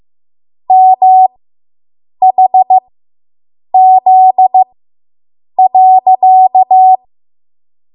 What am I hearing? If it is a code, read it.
Morse code, heard as a keyed tone, about 15 wpm, spaced as for 7 wpm, 750 Hz. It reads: MHZ.